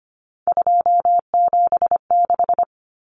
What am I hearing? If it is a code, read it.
Morse code, heard as a keyed tone, about 25 wpm, 700 Hz. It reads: 276
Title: Common Cut Numbers - 3-Digit 25wpm